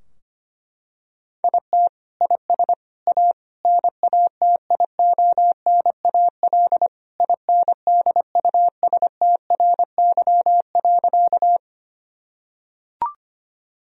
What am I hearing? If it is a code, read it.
Morse code, heard as a keyed tone, about 25 wpm, 700 Hz. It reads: IT IS A NATIONAL INDUSTRY.
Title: Sentences from Top 1000 Words - Review Part 4 - 25wpm